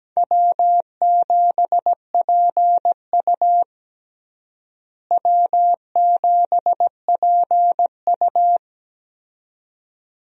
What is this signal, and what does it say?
Morse code, heard as a keyed tone, about 17 wpm, 700 Hz. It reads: W7PU W7PU
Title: USA Call Signs - 4 Characters - Repeated 2x - 17wpm